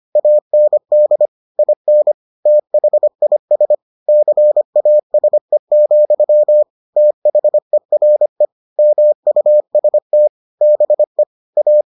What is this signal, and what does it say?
Morse code, heard as a keyed tone, about 25 wpm, 600 Hz. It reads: AND IN THIS CASE, THERE MUST BE A